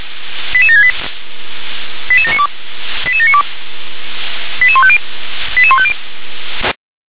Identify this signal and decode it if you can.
Five-tone selective call (ZVEI), heard as quick runs of five tones, tones of about 70 ms. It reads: 88768 88792 88792 88150 88150